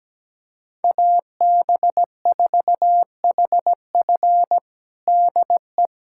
Morse code, heard as a keyed tone, about 17 wpm, 700 Hz. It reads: AB4HF DE
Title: USA Callsign DE Callsign - 5 Characters - 17wpm